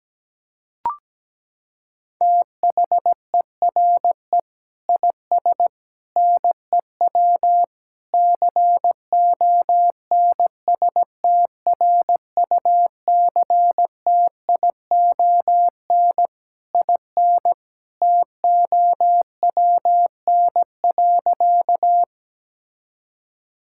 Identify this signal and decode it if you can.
Morse code, heard as a keyed tone, about 17 wpm, 700 Hz. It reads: THERE IS NEW CONSTRUCTION IN TOWN.
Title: Sentences from Top 1600 Words 17wpm